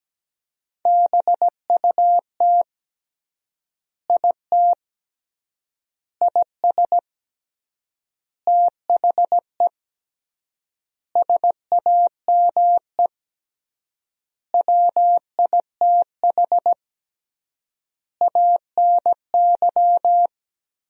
Morse code, heard as a keyed tone, about 17 wpm, 700 Hz. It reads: BUT IT IS THE SAME WITH ANY
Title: 3x Word-Spacing - Sentences from Top 300 Words - 17wpm